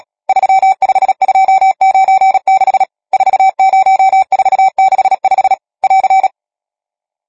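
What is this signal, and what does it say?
Morse code, heard as a keyed tone, about 37 wpm, 770 Hz. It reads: 35296 40465 <AR>
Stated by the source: Morse code number practice, machine-sent training text (Nácvik telegrafie, OK1FCB).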